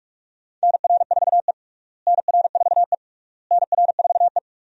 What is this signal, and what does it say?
Morse code, heard as a keyed tone, about 46 wpm, 700 Hz. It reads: NR4E NR4E NR4E
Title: USA Call Signs - 4 Characters - Repeated 3x - 45wpm